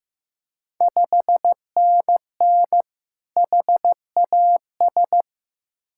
Morse code, heard as a keyed tone, about 15 wpm, 700 Hz. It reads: 5NN HAS
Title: Ontario QSO Party Exchange 15wpm